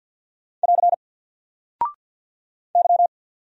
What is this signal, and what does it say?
Morse code, heard as a keyed tone, about 50 wpm, 700 Hz. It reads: <AR> Y
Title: Single Character (Letter:Number:Punctuation:Prosign) 50wpm